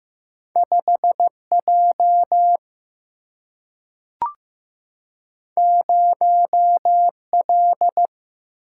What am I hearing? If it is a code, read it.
Morse code, heard as a keyed tone, about 15 wpm, 700 Hz. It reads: 5 J 0 L
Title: Single-Letter Single-Number 15wpm